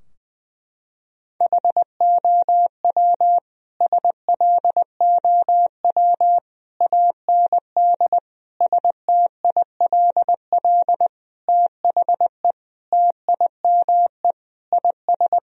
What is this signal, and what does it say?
Morse code, heard as a keyed tone, about 21 wpm, 700 Hz. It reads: HOW SLOW AND STILL THE TIME IS
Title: Sentences from Top 1000 Words - Review Part 2 - 20wpm